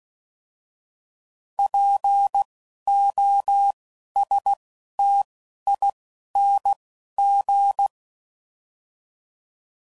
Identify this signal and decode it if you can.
Morse code, heard as a keyed tone, about 16 wpm, 800 Hz. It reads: P O S T I N G